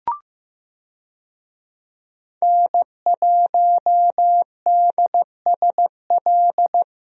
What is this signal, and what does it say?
Morse code, heard as a keyed tone, about 15 wpm, 700 Hz. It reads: N1DSL
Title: USA Call Signs - 5 Characters - Repeated 2x - 15wpm